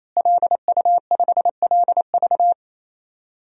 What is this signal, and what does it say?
Morse code, heard as a keyed tone, about 27 wpm, 700 Hz. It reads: LU5LV
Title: DXCC-100 Call Signs - 5 Characters - Repeated 3x - 28wpm